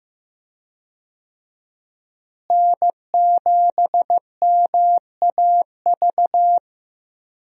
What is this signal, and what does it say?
Morse code, heard as a keyed tone, about 15 wpm, 700 Hz. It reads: N7MAV
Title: USA Call Signs - 5 Characters - Repeated 3x - 15wpm